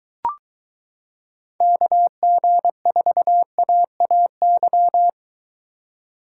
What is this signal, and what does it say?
Morse code, heard as a keyed tone, about 23 wpm, 700 Hz. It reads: KG4AAY